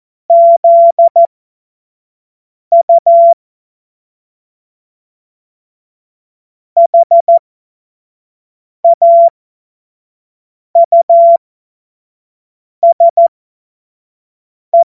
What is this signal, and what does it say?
Morse code, heard as a keyed tone, about 14 wpm, spaced as for 5 wpm, 680 Hz. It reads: ZU HAUSE
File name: Morsekurs / Witze / Witz 21. Februar 2023 25 F70